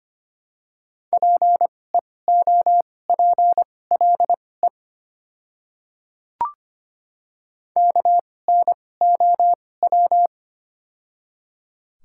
Morse code, heard as a keyed tone, about 25 wpm, spaced as for 18 wpm, 690 Hz. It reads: PEOPLE KNOW
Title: 2x Character Spacing - Top 100 Words - 25wpm